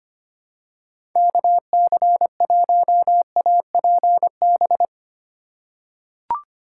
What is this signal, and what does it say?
Morse code, heard as a keyed tone, about 25 wpm, 700 Hz. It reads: KC1APB
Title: USA Call Signs - 6 Characters - 4x Character Spacing - 25wpm